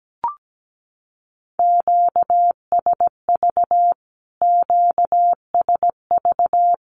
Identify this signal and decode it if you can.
Morse code, heard as a keyed tone, about 17 wpm, 700 Hz. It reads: QSV QSV
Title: Q-Codes Repeated 3x 17wpm